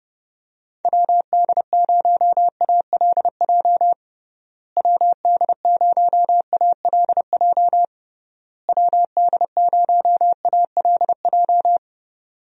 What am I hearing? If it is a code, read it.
Morse code, heard as a keyed tone, about 32 wpm, 700 Hz. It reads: WD0ALJ WD0ALJ WD0ALJ
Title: USA Call Signs - 6 Characters - Repeated 3x - 30wpm